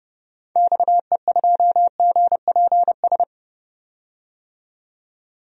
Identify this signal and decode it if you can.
Morse code, heard as a keyed tone, about 30 wpm, 700 Hz. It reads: XE2GPS
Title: DXCC-100 Call Signs - 6 Characters - 30wpm